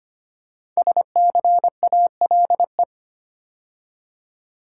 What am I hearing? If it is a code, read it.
Morse code, heard as a keyed tone, about 25 wpm, 700 Hz. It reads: SCALE
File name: 5 Letter Words - First 3 Letters Given - 25wpm